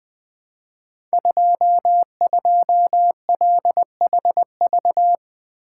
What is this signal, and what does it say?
Morse code, heard as a keyed tone, about 20 wpm, 700 Hz. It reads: 22LHV
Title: Code Groups - 5 Letter:Numbers - 20wpm